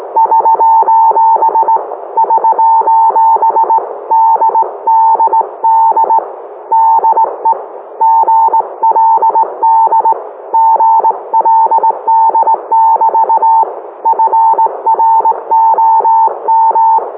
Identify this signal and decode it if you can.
Morse code, heard as a keyed tone, about 18 wpm, 900 Hz. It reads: <SOS> <SOS> DDD DE GLD GLD<BT> FROM